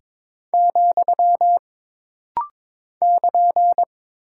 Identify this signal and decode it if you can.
Morse code, heard as a keyed tone, about 22 wpm, 700 Hz. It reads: , <KN>
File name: practice / 22wpm / Single Punctuation-Prosign - Rapid-Fire - 22wpm